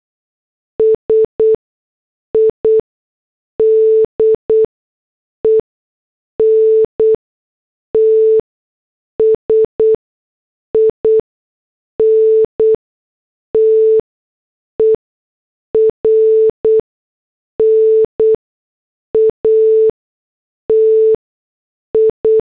Morse code, heard as a keyed tone, about 8 wpm, spaced as for 6 wpm, 440 Hz. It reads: SIDENTS INTERNATI